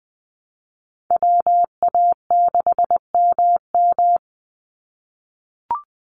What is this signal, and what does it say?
Morse code, heard as a keyed tone, about 20 wpm, 700 Hz. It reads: WA6MM
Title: NA SOTA Chaser Callsigns 20wpm